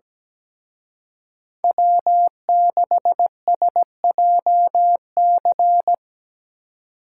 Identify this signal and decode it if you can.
Morse code, heard as a keyed tone, about 19 wpm, 700 Hz. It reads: W6SJC
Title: Speed Racing - USA Call Signs 17wpm